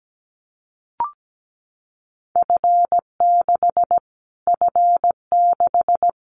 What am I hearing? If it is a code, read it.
Morse code, heard as a keyed tone, about 17 wpm, 700 Hz. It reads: F6 F6
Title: Callsign Prefixes for Top 25 DXCC - Repeated 3x - 17wpm